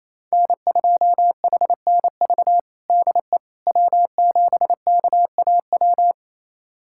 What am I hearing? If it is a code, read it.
Morse code, heard as a keyed tone, about 28 wpm, 700 Hz. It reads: N2HNV DE W7KAW
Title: USA Callsign DE Callsign - 5 Characters - 28wpm